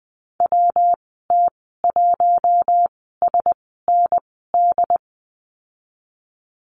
Morse code, heard as a keyed tone, about 20 wpm, spaced as for 15 wpm, 700 Hz. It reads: WT1SND
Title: 128 Lesson 21g - Number 6 - Call Signs 20wpm